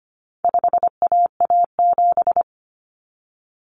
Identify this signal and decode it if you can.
Morse code, heard as a keyed tone, about 25 wpm, 700 Hz. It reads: 5AA7
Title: Common Cut Numbers - 4-Digit 25wpm